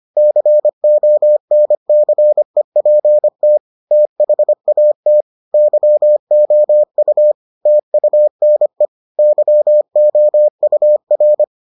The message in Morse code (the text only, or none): CONCEPT THAT YOU TUNE YOUR